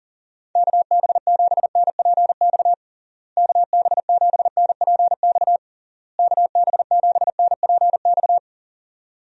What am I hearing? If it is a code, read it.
Morse code, heard as a keyed tone, about 40 wpm, 700 Hz. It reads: KB7NPX KB7NPX KB7NPX
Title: USA Call Signs - 6 Characters - Repeated 3x - 40wpm